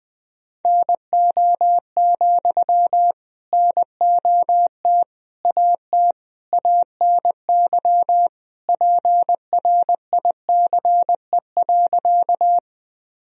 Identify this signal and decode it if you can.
Morse code, heard as a keyed tone, about 20 wpm, 700 Hz. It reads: NO, NOT AT ANY PRICE.
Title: Sentences from Top 1000 Words - Review Part 4 - 20wpm